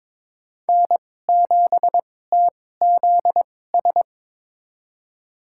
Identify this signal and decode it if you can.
Morse code, heard as a keyed tone, about 22 wpm, spaced as for 16 wpm, 700 Hz. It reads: N7TZS
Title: USA Call Signs - 5 Characters - 2x Character Spacing - 22wpm